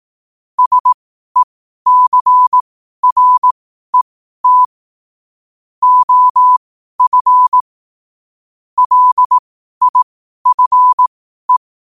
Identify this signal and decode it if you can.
Morse code, heard as a keyed tone, about 18 wpm, spaced as for 12 wpm, 1000 Hz. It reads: SECRET OF LIFE
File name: qotdfiles / 12wpm